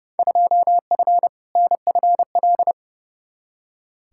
Morse code, heard as a keyed tone, about 32 wpm, 700 Hz. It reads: 2F NFL
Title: Field Day Exchanges - 30wpm